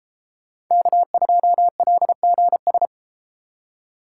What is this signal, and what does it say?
Morse code, heard as a keyed tone, about 33 wpm, 700 Hz. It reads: K2LGS